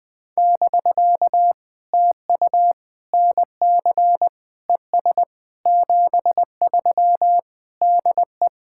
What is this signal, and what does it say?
Morse code, heard as a keyed tone, about 20 wpm, 700 Hz. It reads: <BK> TU NC ES 73 DE
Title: POTA QSO Practice - 20wpm